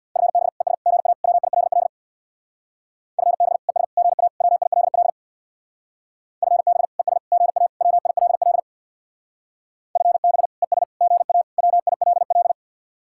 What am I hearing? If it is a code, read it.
Morse code, heard as a keyed tone, about 63 wpm, 700 Hz. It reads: FB ES GA WILL FB ES GA WILL FB ES GA WILL FB ES GA WILL